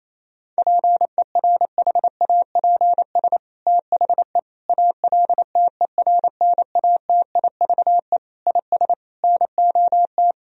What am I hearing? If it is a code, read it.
Morse code, heard as a keyed tone, about 28 wpm, 700 Hz. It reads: PERHAPS THE ALTERNATIVE IS NOT
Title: Sentences from Top 1000 Words - Review Part 4 - 28wpm